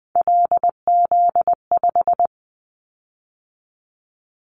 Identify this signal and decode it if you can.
Morse code, heard as a keyed tone, about 20 wpm, 700 Hz. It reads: LZ5
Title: Callsign Prefixes for Top 25 DXCC - 20wpm